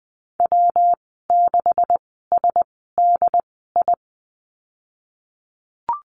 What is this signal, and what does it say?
Morse code, heard as a keyed tone, about 20 wpm, spaced as for 15 wpm, 700 Hz. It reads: W6SDI
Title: 147 Lesson 24g - Letter y - Call Signs 20wpm